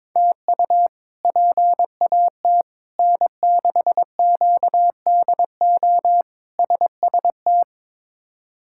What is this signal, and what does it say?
Morse code, heard as a keyed tone, about 22 wpm, 700 Hz. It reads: TU PAT N6QDO SST
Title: K1USN SST QSO Practice - 22wpm